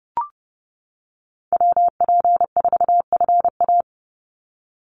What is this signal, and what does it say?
Morse code, heard as a keyed tone, about 30 wpm, 700 Hz. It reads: WP4FA
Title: USA Call Signs 30wpm